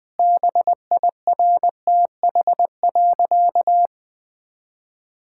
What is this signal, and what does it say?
Morse code, heard as a keyed tone, about 20 wpm, 700 Hz. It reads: BIRTH.
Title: Sentences from Top 1000 Words - Review Part 5 - 20wpm